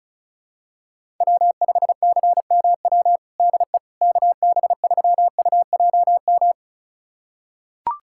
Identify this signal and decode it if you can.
Morse code, heard as a keyed tone, about 36 wpm, 700 Hz. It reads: W5CMW DE KB3UJM
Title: USA Callsign DE Callsign 35wpm